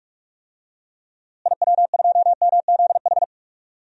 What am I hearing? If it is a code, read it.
Morse code, heard as a keyed tone, about 45 wpm, 700 Hz. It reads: IW2MZH
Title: DXCC-100 Call Signs - 6 Characters - Repeated 3x - 45wpm